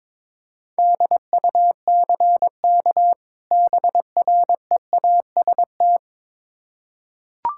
Morse code, heard as a keyed tone, about 22 wpm, 700 Hz. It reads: DUCK BREAST
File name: Difficult Words with Hints - 22wpm